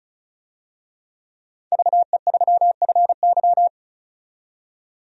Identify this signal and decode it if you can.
Morse code, heard as a keyed tone, about 34 wpm, 700 Hz. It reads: VE3FY
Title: DXCC-100 Call Signs - 5 Characters - 35wpm